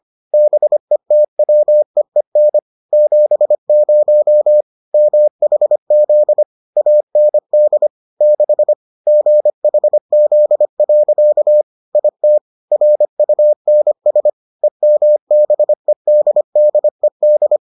Morse code, heard as a keyed tone, about 25 wpm, 600 Hz. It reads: BETWEEN 70 MHZ AND 6 GHZ. IT RUNS EMBEDDED